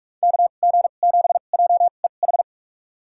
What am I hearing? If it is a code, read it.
Morse code, heard as a keyed tone, about 46 wpm, spaced as for 33 wpm, 700 Hz. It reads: KG7JEH
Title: USA Call Signs - 6 Characters - 2x Character Spacing - 45wpm